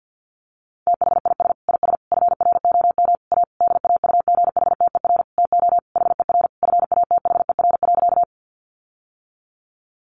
Morse code, heard as a keyed tone, about 50 wpm, 700 Hz. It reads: THIS IS FROM A DAUGHTER TO HER FATHER.
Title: Sentences from Top 1500 Words 50wpm